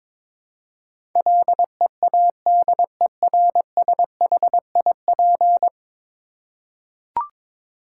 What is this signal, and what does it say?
Morse code, heard as a keyed tone, about 22 wpm, 700 Hz. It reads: LEADERSHIP
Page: morse-code-ninja-practice-22wpm